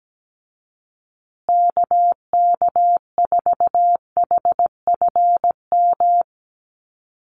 Morse code, heard as a keyed tone, about 17 wpm, 700 Hz. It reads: KK4HFM
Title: USA Call Signs Repeated 3x 17wpm